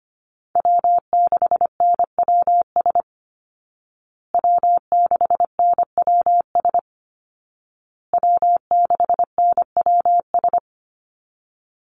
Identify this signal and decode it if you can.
Morse code, heard as a keyed tone, about 25 wpm, 700 Hz. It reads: W6NWS W6NWS W6NWS
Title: Call Signs Repeated 2x - 25wpm